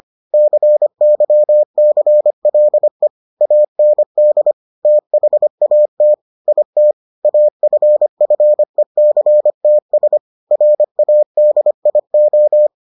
Morse code, heard as a keyed tone, about 25 wpm, 600 Hz. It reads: CYCLE AND THAT IT AFFECTS RADIO